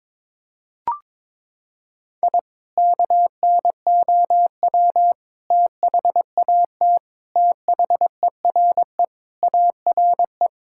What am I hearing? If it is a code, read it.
Morse code, heard as a keyed tone, about 22 wpm, 700 Hz. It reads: I KNOW THAT THERE ARE
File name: practice / 22wpm / Sentences with 3-Word Phrases - 22wpm